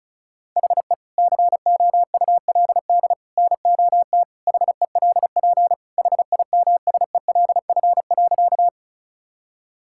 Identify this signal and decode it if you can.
Morse code, heard as a keyed tone, about 36 wpm, 700 Hz. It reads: HE COULD NOT HELP HIMSELF.
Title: Sentences from Top 1600 Words 35wpm